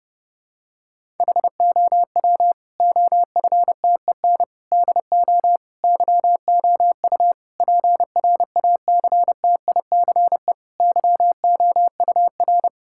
Morse code, heard as a keyed tone, about 30 wpm, 700 Hz. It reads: HOW OFTEN DO YOU PRACTICE YOUR
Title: Sentences from Top 1000 Words - Review Part 6 - 30wpm